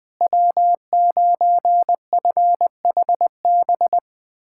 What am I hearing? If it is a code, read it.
Morse code, heard as a keyed tone, about 20 wpm, 700 Hz. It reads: W9FHB
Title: USA Callsign DE Callsign - 5 Characters - 20wpm